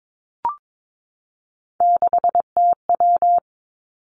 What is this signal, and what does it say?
Morse code, heard as a keyed tone, about 22 wpm, 700 Hz. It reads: 6TW
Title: Three Challenging Letter-Number Combinations 22wpm